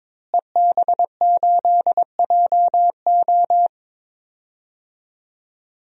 Morse code, heard as a keyed tone, about 22 wpm, 700 Hz. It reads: EB8JO
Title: DXCC-100 Call Signs - 5 Characters - Repeated 2x - 22wpm